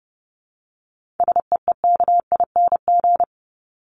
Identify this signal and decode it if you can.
Morse code, heard as a keyed tone, about 30 wpm, 700 Hz. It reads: SEEKING
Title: ING Suffix - Top 100 Words with Base Words 30wpm